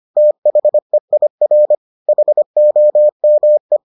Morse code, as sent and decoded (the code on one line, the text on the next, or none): - .... .. .-. / .... --- -- .
THEIR HOME